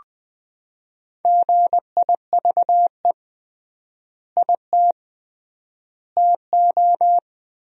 Morse code, heard as a keyed tone, about 21 wpm, 700 Hz. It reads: GIVE IT TO
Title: 3x Word-Spacing - Sentences from Top 200 Words - 20wpm